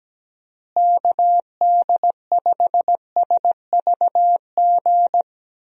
Morse code, heard as a keyed tone, about 17 wpm, 700 Hz. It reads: KD5SVG
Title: USA Call Signs - 6 Characters - Repeated 2x - 17wpm